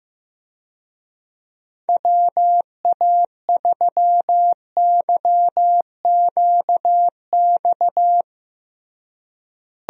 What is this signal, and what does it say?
Morse code, heard as a keyed tone, about 15 wpm, 700 Hz. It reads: WA3YQX
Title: USA Call Signs - 6 Characters - Repeated 2x - 15wpm